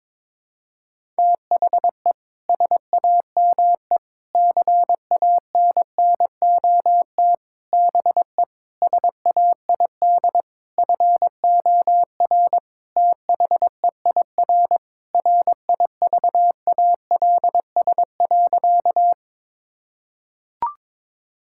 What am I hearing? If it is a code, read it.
Morse code, heard as a keyed tone, about 22 wpm, 700 Hz. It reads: THE SAME CANNOT BE SAID FOR THEIR RIVALS.
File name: Sentences with 7-Word Phrases - 22wpm